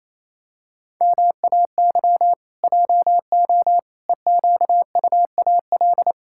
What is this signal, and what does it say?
Morse code, heard as a keyed tone, about 27 wpm, 700 Hz. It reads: MAY JO EQUAL
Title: Pangram Challenge v2 - 28wpm